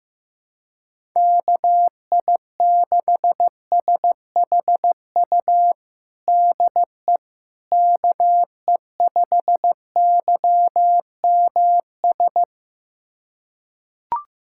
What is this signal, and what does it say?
Morse code, heard as a keyed tone, about 15 wpm, 700 Hz. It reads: KI6SHU DE KE5YMS
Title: USA Callsign DE Callsign - 6 Characters - 15wpm